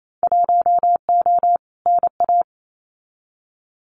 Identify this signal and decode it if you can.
Morse code, heard as a keyed tone, about 28 wpm, 700 Hz. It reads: 1O NA
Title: Winter Field Day Exchanges 28wpm